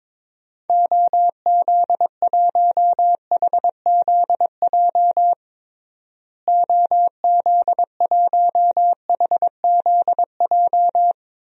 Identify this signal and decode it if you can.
Morse code, heard as a keyed tone, about 22 wpm, 700 Hz. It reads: OZ1HZJ OZ1HZJ
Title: DXCC-100 Call Signs - 6 Characters - Repeated 3x - 22wpm